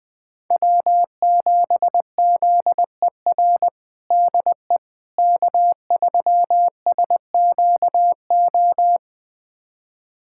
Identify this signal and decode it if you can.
Morse code, heard as a keyed tone, about 20 wpm, 700 Hz. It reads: W7ZER DE K3SQO
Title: USA Callsign DE Callsign - 5 Characters - 20wpm